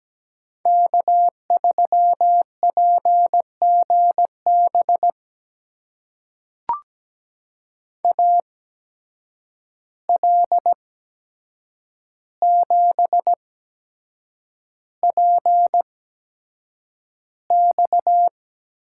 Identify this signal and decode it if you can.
Morse code, heard as a keyed tone, about 17 wpm, spaced as for 7 wpm, 700 Hz. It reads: K3PGB AL7PX
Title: USA Call Signs - 5 Characters - 8x Character Spacing - 17wpm